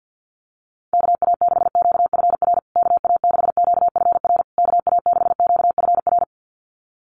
Morse code, heard as a keyed tone, about 50 wpm, 700 Hz. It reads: KA6QFR KA6QFR KA6QFR
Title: USA Call Signs Repeated 3x 50wpm